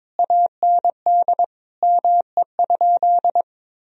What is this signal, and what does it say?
Morse code, heard as a keyed tone, about 22 wpm, 700 Hz. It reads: AND ME?
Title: Sentences from Top 1000 Words - Review Part 2 - 22wpm